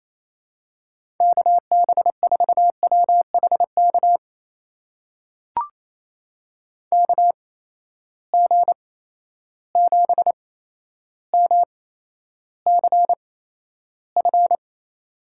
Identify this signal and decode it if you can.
Morse code, heard as a keyed tone, about 28 wpm, spaced as for 12 wpm, 700 Hz. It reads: KB4WHK KG7MCF
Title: USA Call Signs - 6 Characters - 8x Character Spacing - 28wpm